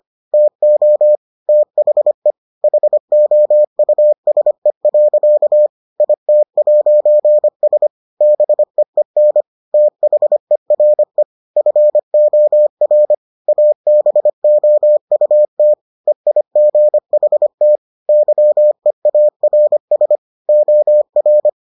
Morse code, heard as a keyed tone, about 25 wpm, 600 Hz. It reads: TO THE HOUSE. IT'S BEEN THERE FOR ABOUT EIGHT YEARS OR